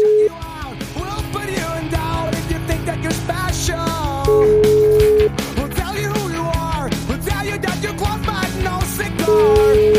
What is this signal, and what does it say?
Ringback tone (Europe, 425 Hz), with noise.